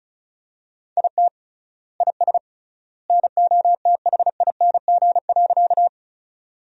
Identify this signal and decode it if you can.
Morse code, heard as a keyed tone, about 36 wpm, 700 Hz. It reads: IT IS NOTHING.
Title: 3x Word-Spacing - Sentences from Top 500 Words - 35wpm